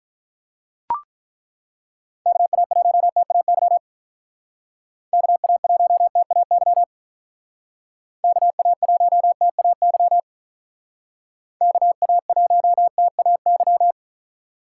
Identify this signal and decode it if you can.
Morse code, heard as a keyed tone, about 43 wpm, 700 Hz. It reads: KA1TAY KA1TAY KA1TAY KA1TAY
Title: Speed Racing - USA 2x3 Call Signs 35wpm